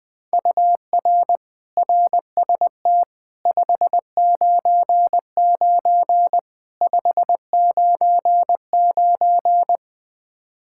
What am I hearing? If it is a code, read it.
Morse code, heard as a keyed tone, about 20 wpm, 700 Hz. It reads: UR RST 599 599
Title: K1USN SST QSO Practice 20wpm